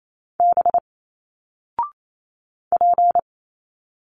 Morse code, heard as a keyed tone, about 28 wpm, 700 Hz. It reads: B P
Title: Single Letters 28wpm